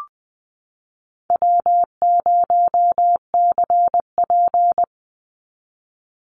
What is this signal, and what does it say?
Morse code, heard as a keyed tone, about 20 wpm, 700 Hz. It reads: W0CP
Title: NA SOTA Activator Callsigns 20wpm